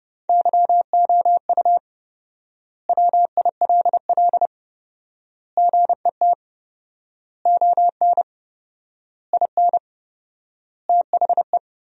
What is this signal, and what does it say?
Morse code, heard as a keyed tone, about 32 wpm, 700 Hz. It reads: YOU WILL GET ON IN THE
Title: 4x Word-Spacing - Sentences from Top 200 Words - 30wpm